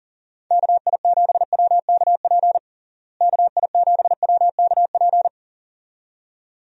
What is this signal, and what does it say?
Morse code, heard as a keyed tone, about 40 wpm, 700 Hz. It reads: KI7WKP KI7WKP
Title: USA Call Signs - 6 Characters - Repeated 2x - 40wpm